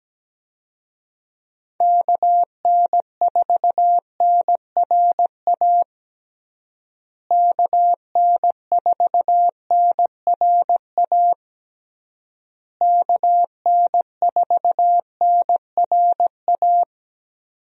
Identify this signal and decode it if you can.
Morse code, heard as a keyed tone, about 17 wpm, 700 Hz. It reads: KN4NRA KN4NRA KN4NRA
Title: USA Call Signs - 6 Characters - Repeated 3x - 17wpm